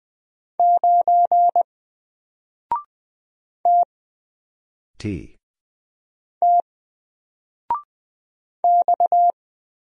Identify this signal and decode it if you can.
Morse code, heard as a keyed tone, about 20 wpm, 700 Hz. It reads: 9 T T X
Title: Single Character (Letter:Number:Punctuation:Prosign) 20wpm